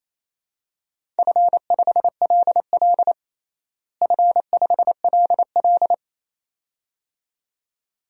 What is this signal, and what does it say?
Morse code, heard as a keyed tone, about 28 wpm, 700 Hz. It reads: F5LL F5LL
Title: DXCC-100 Call Signs - 4 Characters - Repeated 2x - 28wpm